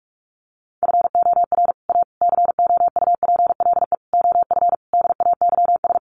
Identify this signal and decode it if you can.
Morse code, heard as a keyed tone, about 44 wpm, 700 Hz. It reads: FOR A COUPLE OF DAYS
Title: Sets of 5 Words Encore2 45wpm